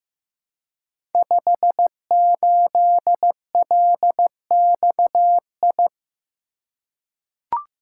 Morse code, heard as a keyed tone, about 15 wpm, 700 Hz. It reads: 58LXI
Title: Code Groups - 5 Letter:Numbers - 15wpm